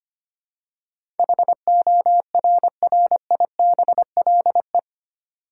Morse code, heard as a keyed tone, about 25 wpm, 700 Hz. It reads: HORRIBLE
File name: Eight Letter Words - 25wpm